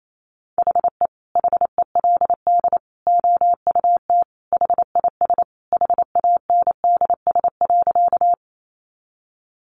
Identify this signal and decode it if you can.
Morse code, heard as a keyed tone, about 28 wpm, 700 Hz. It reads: HE HELD OUT HIS HANDS.
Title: A Sound Of Thunder 28wpm